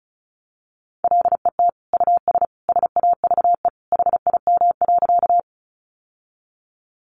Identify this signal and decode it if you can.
Morse code, heard as a keyed tone, about 34 wpm, 700 Hz. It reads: LET US SAVE HIM.
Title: Sentences from Top 1100 Words 35wpm